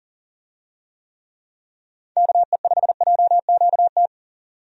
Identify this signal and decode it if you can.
Morse code, heard as a keyed tone, about 40 wpm, 700 Hz. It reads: KE5JQT
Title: USA Call Signs - 6 Characters - 40wpm